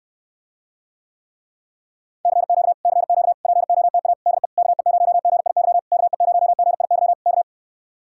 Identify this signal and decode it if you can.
Morse code, heard as a keyed tone, about 67 wpm, 700 Hz. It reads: CQ CQ POTA DE KE0GIY KE0GIY K